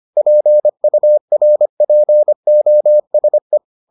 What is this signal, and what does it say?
Morse code, heard as a keyed tone, about 25 wpm, 600 Hz. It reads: PURPOSE